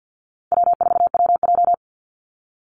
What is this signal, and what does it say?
Morse code, heard as a keyed tone, about 50 wpm, 700 Hz. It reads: W4WJ
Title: Easy Call Signs - 50wpm